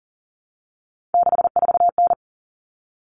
Morse code, heard as a keyed tone, about 40 wpm, 700 Hz. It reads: 64N
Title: Common Cut Numbers - 3-Digit 40wpm